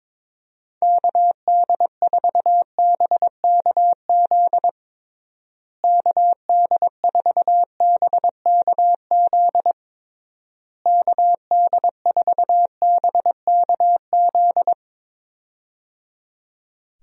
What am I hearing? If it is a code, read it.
Morse code, heard as a keyed tone, about 22 wpm, 700 Hz. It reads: KD4BKZ KD4BKZ KD4BKZ
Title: USA Call Signs - 6 Characters - Repeated 3x - 22wpm